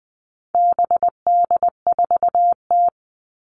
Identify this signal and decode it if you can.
Morse code, heard as a keyed tone, about 20 wpm, 700 Hz. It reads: BD4T
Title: All Cut Numbers - 4-Digit 20wpm